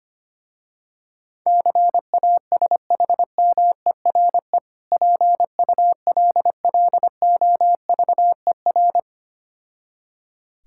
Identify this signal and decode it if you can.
Morse code, heard as a keyed tone, about 25 wpm, 700 Hz. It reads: CASHMERE PULLOVER
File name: practice / 25wpm / Difficult Words with Hints - 25wpm